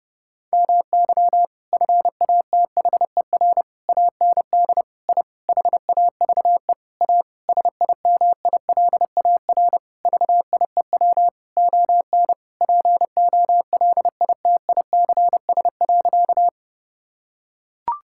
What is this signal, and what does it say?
Morse code, heard as a keyed tone, about 30 wpm, 700 Hz. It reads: MY FATHER AND I HAVE A SIMILAR VIEW ON POLITICS.
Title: Sentences from Top 1000 Words - Review Part 6 - 30wpm